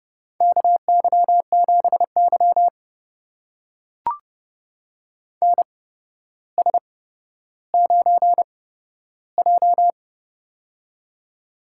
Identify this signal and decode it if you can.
Morse code, heard as a keyed tone, about 30 wpm, spaced as for 13 wpm, 700 Hz. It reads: KY7Y NS9J
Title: USA Call Signs - 4 Characters - 8x Character Spacing - 30wpm